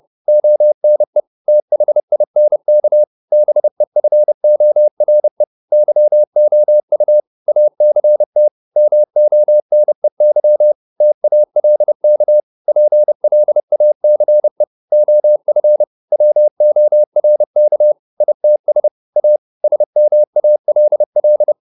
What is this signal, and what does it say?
Morse code, heard as a keyed tone, about 30 wpm, 600 Hz. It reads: ONE THINK BEFORE YOU ACT MONEY TALK PLACE OF WORK ITS A SMALL